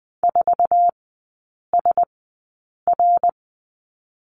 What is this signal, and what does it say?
Morse code, heard as a keyed tone, about 20 wpm, spaced as for 8 wpm, 700 Hz. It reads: 4SR
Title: 053 Lesson 10e - Letter r - Three Characters 20wpm